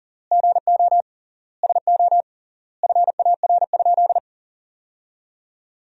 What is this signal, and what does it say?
Morse code, heard as a keyed tone, about 37 wpm, 700 Hz. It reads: GO SO FAR?
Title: 3x Word-Spacing - Sentences from Top 300 Words - 40wpm